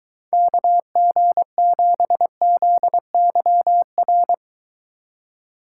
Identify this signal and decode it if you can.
Morse code, heard as a keyed tone, about 21 wpm, 700 Hz. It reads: KG7ZYR